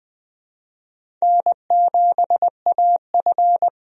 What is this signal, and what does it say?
Morse code, heard as a keyed tone, about 20 wpm, 700 Hz. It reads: N7AF
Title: USA Call Signs - 4 Characters - 20wpm